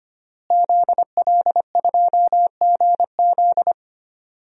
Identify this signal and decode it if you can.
Morse code, heard as a keyed tone, about 25 wpm, 700 Hz. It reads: ZL2GZ
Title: DXCC-100 Call Signs - 5 Characters - Repeated 3x - 25wpm